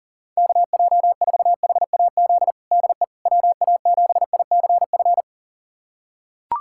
Morse code, heard as a keyed tone, about 40 wpm, 700 Hz. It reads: KJ4HAZ DE WA7ICF
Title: USA Callsign DE Callsign - 6 Characters - 40wpm